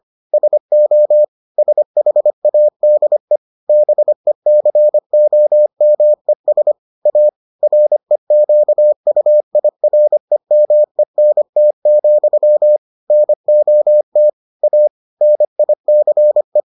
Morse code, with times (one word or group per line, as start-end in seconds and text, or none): 0.3-1.2 SO
1.6-3.4 SHADE
3.7-6.7 BECOMES
7.1-7.3 A
7.6-12.8 REQUIREMENT,
13.1-14.3 NOT
14.6-14.9 A
15.2-16.6 NICE